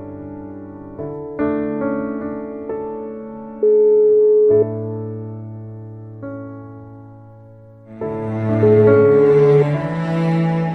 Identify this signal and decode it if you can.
Ringback tone (Europe, 425 Hz).